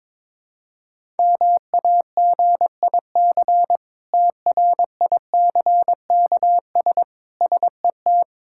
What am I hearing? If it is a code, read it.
Morse code, heard as a keyed tone, about 22 wpm, 700 Hz. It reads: MAGIC TRICKS SET
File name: Difficult Words with Hints - 22wpm